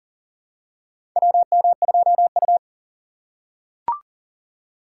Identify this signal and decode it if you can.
Morse code, heard as a keyed tone, about 40 wpm, 690 Hz. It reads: WM2U
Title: USA Call Signs - 4 Characters - 2x Character Spacing - 40wpm